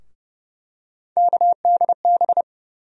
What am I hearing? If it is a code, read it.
Morse code, heard as a keyed tone, about 30 wpm, 700 Hz. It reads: KDB
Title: USA Call Sign Postfixes - 3 Letters 30wpm